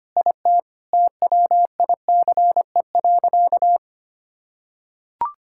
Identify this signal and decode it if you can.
Morse code, heard as a keyed tone, about 25 wpm, 700 Hz. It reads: IT TWICE.
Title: Sentences from Top 1600 Words 25wpm